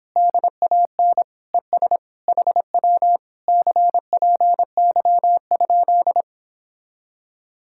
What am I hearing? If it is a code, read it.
Morse code, heard as a keyed tone, about 31 wpm, 700 Hz. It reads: DAN ES HW CPY?